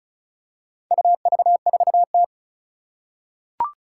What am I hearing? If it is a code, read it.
Morse code, heard as a keyed tone, about 36 wpm, 700 Hz. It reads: UV4T
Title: Four Challenging Letter-Number Combinations 35wpm